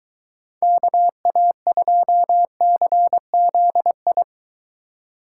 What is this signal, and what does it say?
Morse code, heard as a keyed tone, about 22 wpm, 700 Hz. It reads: KA2CZI